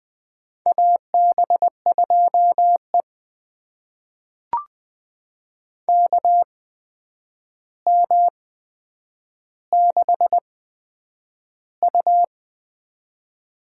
Morse code, heard as a keyed tone, about 20 wpm, spaced as for 9 wpm, 700 Hz. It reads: AB2E KM6U
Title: USA Call Signs - 8x Character Spacing - 20wpm